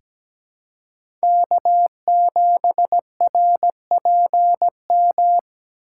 Morse code, heard as a keyed tone, about 17 wpm, 700 Hz. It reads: K7RPM
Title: USA Call Signs - 4x Character Spacing - 17wpm